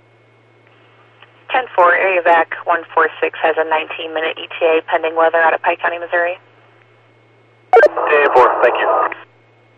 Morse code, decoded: none

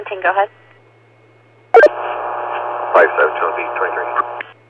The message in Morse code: none

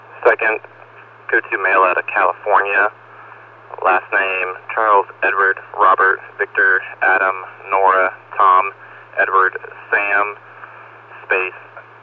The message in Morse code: none

